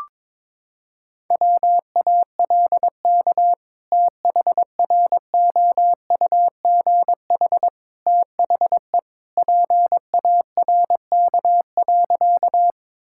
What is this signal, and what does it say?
Morse code, heard as a keyed tone, about 21 wpm, 700 Hz. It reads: WALK THROUGH THE PARK.
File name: practice / 22wpm / Sentences with 2-Word Phrases - 22wpm